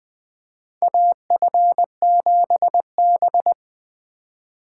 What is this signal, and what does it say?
Morse code, heard as a keyed tone, about 20 wpm, 700 Hz. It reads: AF7B
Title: USA Call Signs - 4 Characters - 3x Character Spacing - 20wpm